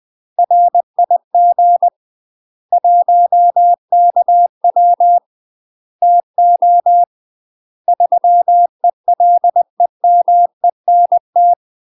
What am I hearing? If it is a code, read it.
Morse code, heard as a keyed tone, about 20 wpm, 700 Hz. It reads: RIG 1KW TO 3ELEMENT
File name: UK MASTER QSOs TRAINING/20WPM